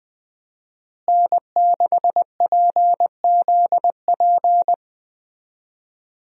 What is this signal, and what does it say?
Morse code, heard as a keyed tone, about 21 wpm, 700 Hz. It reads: N6PZP
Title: USA Call Signs - 5 Characters - Repeated 2x - 20wpm